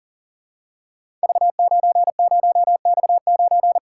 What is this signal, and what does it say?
Morse code, heard as a keyed tone, about 38 wpm, 690 Hz. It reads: V90X9
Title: Code Groups - 5 Letter:Numbers - 40wpm